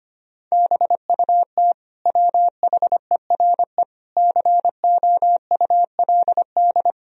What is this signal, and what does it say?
Morse code, heard as a keyed tone, about 25 wpm, 700 Hz. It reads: BUT WHERE COULD
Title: Sentences from Top 1000 Words - Review Part 2 - 25wpm